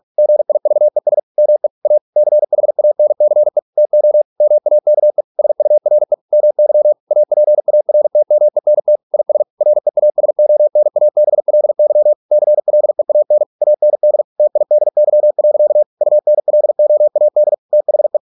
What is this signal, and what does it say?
Morse code, heard as a keyed tone, about 46 wpm, 600 Hz. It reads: GIVES ME A CHANCE TO MAKE SURE MY APARTMENT IS REASONABLY CLEAN AND TIDY. UNLOAD THE